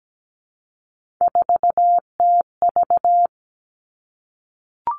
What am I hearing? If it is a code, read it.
Morse code, heard as a keyed tone, about 17 wpm, 700 Hz. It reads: 4TV
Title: Three Challenging Letter-Number Combinations 17wpm